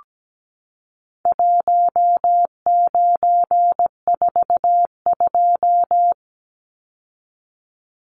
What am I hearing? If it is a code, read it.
Morse code, heard as a keyed tone, about 17 wpm, 700 Hz. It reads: 1942
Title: Numbers - Contest Style - 4-Digit 17wpm